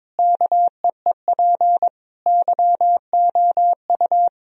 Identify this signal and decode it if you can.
Morse code, heard as a keyed tone, about 22 wpm, 700 Hz. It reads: KEEP YOU
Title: Sentences from Top 1000 Words - Review Part 2 - 22wpm